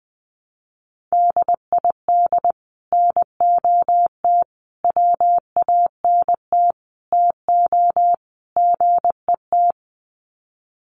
Morse code, heard as a keyed tone, about 19 wpm, 700 Hz. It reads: DID NOT WANT TO GET
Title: Sets of 5 Words Encore2 20wpm